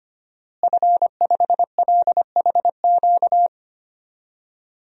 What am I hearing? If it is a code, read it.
Morse code, heard as a keyed tone, about 25 wpm, 700 Hz. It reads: F5LHQ
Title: DXCC-100 Call Signs - 5 Characters - Repeated 3x - 25wpm